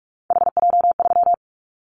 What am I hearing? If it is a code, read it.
Morse code, heard as a keyed tone, about 44 wpm, 700 Hz. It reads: HJ3
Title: Callsign Prefixes for Top 100 DXCC - 45wpm